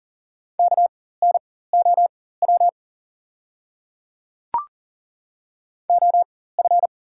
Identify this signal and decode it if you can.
Morse code, heard as a keyed tone, about 40 wpm, spaced as for 19 wpm, 690 Hz. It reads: KNOW OF
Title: 4x Character Spacing - Top 100 Words - 40wpm